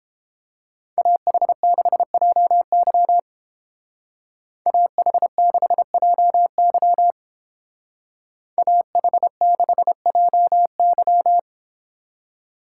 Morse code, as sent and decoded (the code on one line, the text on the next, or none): .- .... -.... .--- -.-- / .- .... -.... .--- -.-- / .- .... -.... .--- -.--
AH6JY AH6JY AH6JY